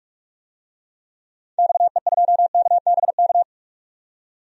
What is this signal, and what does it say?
Morse code, heard as a keyed tone, about 46 wpm, 690 Hz. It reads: XE2KBK
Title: DXCC-100 Call Signs - 6 Characters - 45wpm